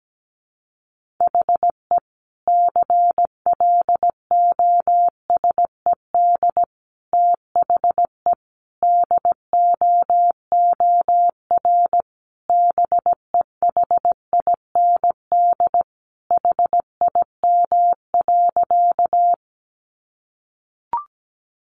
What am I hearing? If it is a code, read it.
Morse code, heard as a keyed tone, about 17 wpm, 700 Hz. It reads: HE CLOSED THE DOOR BEHIND HIM.
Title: Sentences from Top 1400 Words 17wpm